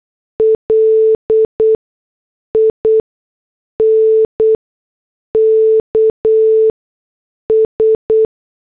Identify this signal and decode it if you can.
Morse code, heard as a keyed tone, about 8 wpm, spaced as for 6 wpm, 440 Hz. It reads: LINKS